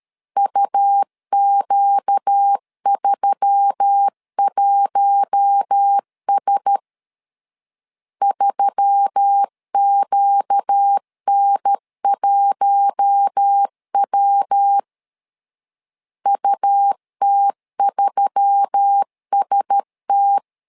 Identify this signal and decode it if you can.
Morse code, heard as a keyed tone, about 13 wpm, 800 Hz. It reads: UQ31S 3QN1W UT3ST
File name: LEZIONI / 4 Q1S3 12wpm 2023